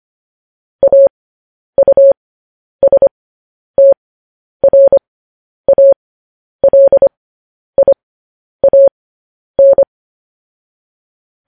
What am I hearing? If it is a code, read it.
Morse code, heard as a keyed tone, about 25 wpm, spaced as for 10 wpm, 550 Hz. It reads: AUSTRALIAN
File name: podcast / News headlines-1x-25x10